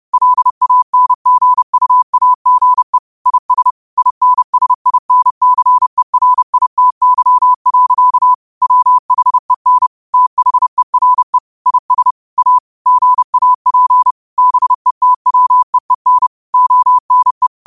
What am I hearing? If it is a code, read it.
Morse code, heard as a keyed tone, about 30 wpm, 1000 Hz. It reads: LANGUAGE IS INSINCERITY. WHEN THERE IS A GAP BETWEEN ONE
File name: qotdfiles / 30wpm